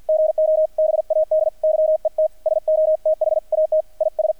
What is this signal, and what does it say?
Morse code, heard as a keyed tone, about 48 wpm, 630 Hz, with radio noise. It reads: GO BAD YET SO THAT IS
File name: archived / CRSnet